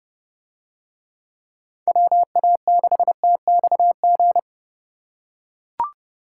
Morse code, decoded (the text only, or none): WA6TXG